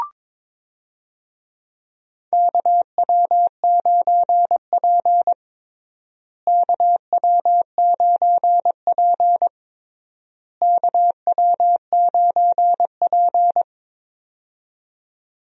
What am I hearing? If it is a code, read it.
Morse code, heard as a keyed tone, about 22 wpm, 700 Hz. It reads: KW9P KW9P KW9P
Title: USA Call Signs - 4 Characters - Repeated 3x - 22wpm